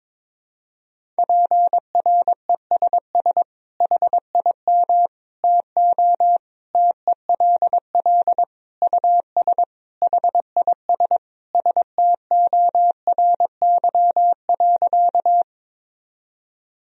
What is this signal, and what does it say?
Morse code, heard as a keyed tone, about 22 wpm, 700 Hz. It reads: PRESS HIM TO TELL US HIS STORY.
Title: Sentences from Top 1000 Words - Review Part 1 - 22wpm